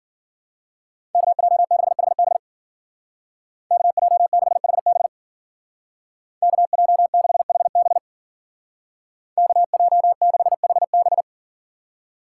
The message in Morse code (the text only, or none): KJ6HB KJ6HB KJ6HB KJ6HB